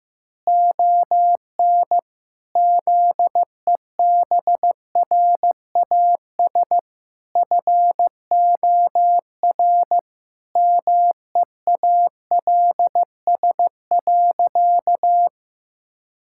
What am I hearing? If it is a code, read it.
Morse code, heard as a keyed tone, about 15 wpm, 700 Hz. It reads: ON ZEBRAS FOR MEALS.
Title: Pangram Challenge v2 - 15wpm